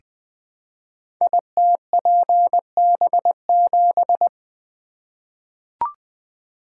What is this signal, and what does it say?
Morse code, heard as a keyed tone, about 20 wpm, 700 Hz. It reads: ITPB7
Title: Code Groups - 5 Letter:Numbers - 20wpm